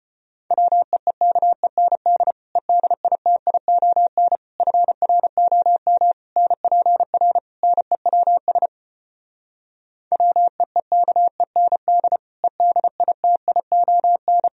Morse code, heard as a keyed tone, about 35 wpm, 700 Hz. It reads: WEEKEND EDITION FROM NPR NEWS WEEKEND EDITION